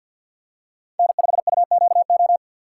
Morse code, heard as a keyed tone, about 53 wpm, 690 Hz. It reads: N5UQO
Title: USA Call Signs - 5 Characters - 2x Character Spacing - 50wpm